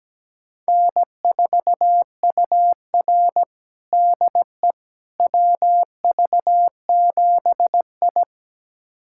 Morse code, heard as a keyed tone, about 17 wpm, 700 Hz. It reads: N4UR DE WV7I
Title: USA Callsign DE Callsign - 4 Characters - 17wpm